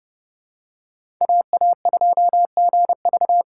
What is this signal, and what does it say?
Morse code, heard as a keyed tone, about 30 wpm, 700 Hz. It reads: AA2GV
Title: USA Call Signs - 5 Characters - 4x Character Spacing - 30wpm